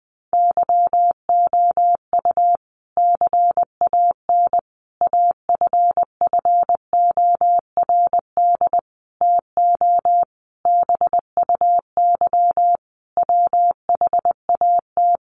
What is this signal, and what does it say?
Morse code, heard as a keyed tone, about 20 wpm, 700 Hz. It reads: YOU CAN AFFORD TO BUY WHAT
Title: Sentences from Top 1100 Words 20wpm